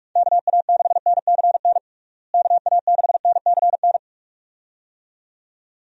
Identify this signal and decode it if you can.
Morse code, heard as a keyed tone, about 46 wpm, 700 Hz. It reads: KA6NCN KA6NCN
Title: USA Call Signs - 6 Characters - Repeated 2x - 45wpm